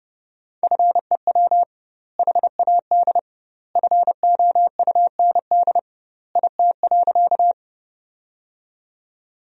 Morse code, heard as a keyed tone, about 30 wpm, 700 Hz. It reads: FEW HAD FOUND IT.
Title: 2x Word-Spacing - Sentences from Top 300 Words - 30wpm